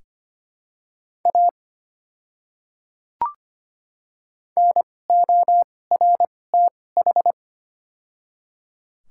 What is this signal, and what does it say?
Morse code, heard as a keyed tone, about 25 wpm, spaced as for 18 wpm, 700 Hz. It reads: A NORTH
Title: 2x Character Spacing - Top 300 Words - 25wpm